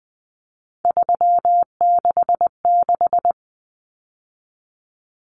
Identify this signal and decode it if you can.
Morse code, heard as a keyed tone, about 20 wpm, 700 Hz. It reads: 366
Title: Numbers - 3-Digit 20wpm